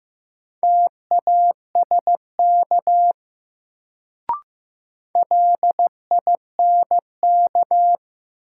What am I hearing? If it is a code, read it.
Morse code, heard as a keyed tone, about 15 wpm, 700 Hz. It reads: TASK LINK
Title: Sentences from Top 1000 Words - Review Part 7 - New Word Drill - 15wpm